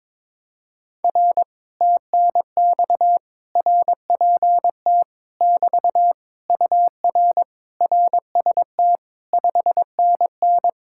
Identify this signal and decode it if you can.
Morse code, heard as a keyed tone, about 22 wpm, 700 Hz. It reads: R TNX RPT <BT> UR RST 5NN
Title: Speed-Racing Common QSO Phrases - 22wpm